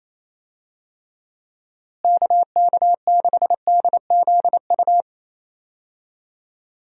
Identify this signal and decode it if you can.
Morse code, heard as a keyed tone, about 28 wpm, 700 Hz. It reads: KK6DZU
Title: USA Call Signs - 6 Characters - Repeated 3x - 28wpm